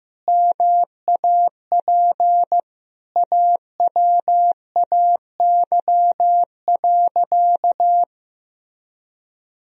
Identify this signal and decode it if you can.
Morse code, heard as a keyed tone, about 15 wpm, 700 Hz. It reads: MAP AWAY.
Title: Sentences from Top 1000 Words - Review Part 2 - 15wpm